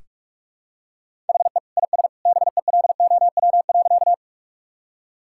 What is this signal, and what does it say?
Morse code, heard as a keyed tone, about 46 wpm, 700 Hz. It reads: HE IS BELOW.
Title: Sentences from Top 1000 Words - Review Part 5 - 45wpm